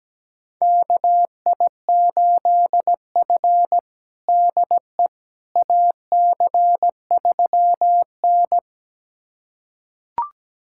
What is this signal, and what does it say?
Morse code, heard as a keyed tone, about 17 wpm, 700 Hz. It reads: KI8F DE AC3N
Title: USA Callsign DE Callsign - 4 Characters - 17wpm